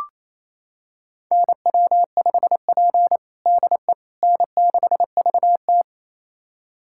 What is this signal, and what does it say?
Morse code, heard as a keyed tone, about 28 wpm, 690 Hz. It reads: NW5P DE N6VT
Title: USA Callsign DE Callsign - 4 Characters - 28wpm